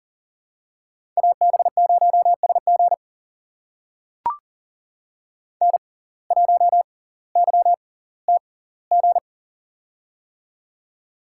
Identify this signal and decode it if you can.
Morse code, heard as a keyed tone, about 40 wpm, spaced as for 20 wpm, 700 Hz. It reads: AB0SG N1YTG
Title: USA Call Signs - 5 Characters - 6x Character Spacing - 40wpm